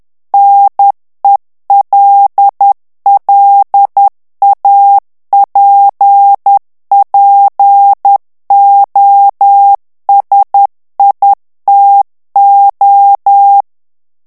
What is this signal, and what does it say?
Morse code, heard as a keyed tone, about 10 wpm, 800 Hz. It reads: NELLAPPOSITO
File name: ITALIEN texte clair 2 vitesse 10 mots minute